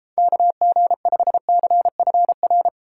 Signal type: Morse code

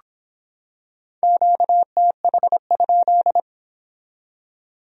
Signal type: Morse code

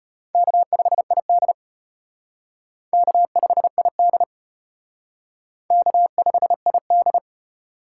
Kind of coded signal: Morse code

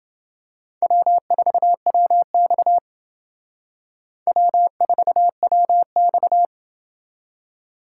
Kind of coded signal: Morse code

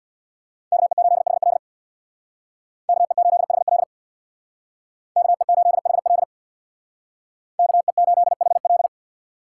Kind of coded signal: Morse code

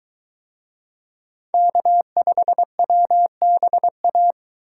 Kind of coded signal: Morse code

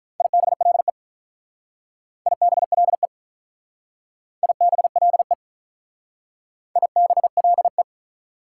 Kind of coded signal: Morse code

